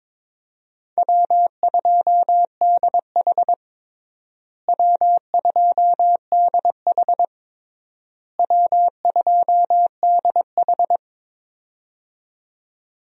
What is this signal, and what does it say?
Morse code, heard as a keyed tone, about 22 wpm, 700 Hz. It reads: W2DH W2DH W2DH
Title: USA Call Signs - 4 Characters - Repeated 3x - 22wpm